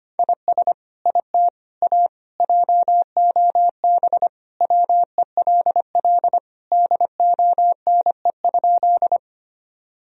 Morse code, heard as a keyed tone, about 25 wpm, 700 Hz. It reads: IS IT A JOB WELL DONE?
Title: Sentences from Top 1000 Words - Review Part 3 - 25wpm